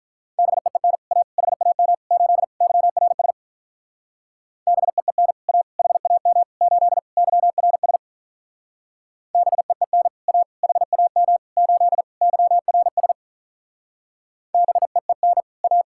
Morse code, heard as a keyed tone, about 45 wpm, 700 Hz. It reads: BEEN A HAM 8 YRS BEEN A HAM 8 YRS BEEN A HAM 8 YRS BEEN A